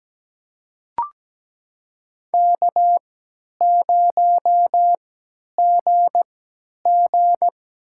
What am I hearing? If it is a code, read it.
Morse code, heard as a keyed tone, about 17 wpm, spaced as for 10 wpm, 700 Hz. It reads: K0GG
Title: USA Call Signs - 4 Characters - 3x Character Spacing - 17wpm